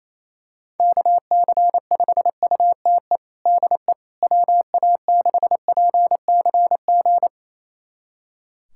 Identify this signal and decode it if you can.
Morse code, heard as a keyed tone, about 28 wpm, 700 Hz. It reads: KC5UTE DE WA6PCG
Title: USA Callsign DE Callsign 28wpm